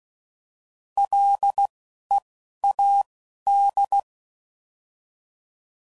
Morse code, heard as a keyed tone, about 16 wpm, spaced as for 11 wpm, 800 Hz. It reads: LEAD